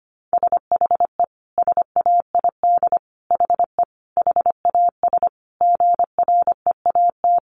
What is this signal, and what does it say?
Morse code, heard as a keyed tone, about 25 wpm, 700 Hz. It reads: SHE SAID HE HAS GREAT
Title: Sentences from Top 1300 Words 25wpm